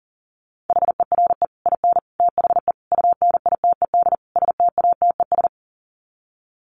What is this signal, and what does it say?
Morse code, heard as a keyed tone, about 40 wpm, 700 Hz. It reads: HERE IN THE UNITED STATES
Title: Sets of 5 Words 40wpm